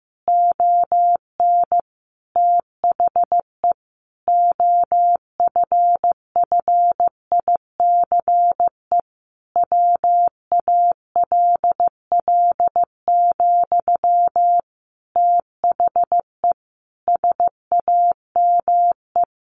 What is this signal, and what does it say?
Morse code, heard as a keyed tone, about 15 wpm, 700 Hz. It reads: ON THE OFFICE WALL, THE SAME
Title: A Sound Of Thunder 15wpm